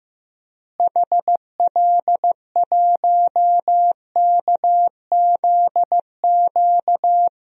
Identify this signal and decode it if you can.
Morse code, heard as a keyed tone, about 15 wpm, 700 Hz. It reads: HL1KZQ
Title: DXCC-100 Call Signs - 6 Characters - Repeated 2x - 15wpm